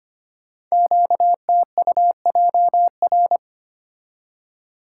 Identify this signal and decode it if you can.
Morse code, heard as a keyed tone, about 26 wpm, 700 Hz. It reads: QTUJR
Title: Code Groups - 5 Letters - 25wpm